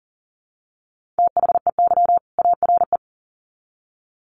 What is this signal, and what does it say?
Morse code, heard as a keyed tone, about 40 wpm, 700 Hz. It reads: THEY ARE
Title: Sets of 2 Words 40wpm